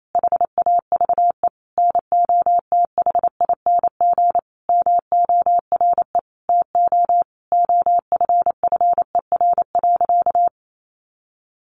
Morse code, heard as a keyed tone, about 28 wpm, 700 Hz. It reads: HAVE NOTHING MORE TO OFFER.
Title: Sentences from Top 1100 Words 28wpm